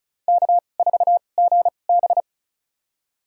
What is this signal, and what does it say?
Morse code, heard as a keyed tone, about 36 wpm, spaced as for 25 wpm, 700 Hz. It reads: K4GB
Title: USA Call Signs - 4 Characters - 2x Character Spacing - 35wpm